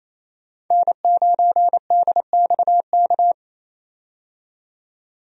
Morse code, heard as a keyed tone, about 28 wpm, 700 Hz. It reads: N9DXK
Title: USA Call Signs - 5 Characters - 28wpm